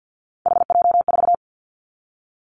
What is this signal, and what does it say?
Morse code, heard as a keyed tone, about 50 wpm, 700 Hz. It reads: HJ4
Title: Callsign Prefixes for Top 100 DXCC - 50wpm